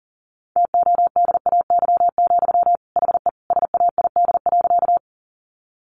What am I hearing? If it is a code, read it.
Morse code, heard as a keyed tone, about 40 wpm, 700 Hz. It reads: TODAY, HE SAID.
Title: Sentences from Top 600 Words 40wpm